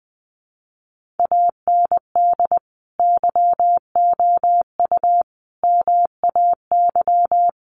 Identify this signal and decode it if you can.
Morse code, heard as a keyed tone, about 20 wpm, 700 Hz. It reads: AND YOU MAY
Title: Sentences from Top 1500 Words 20wpm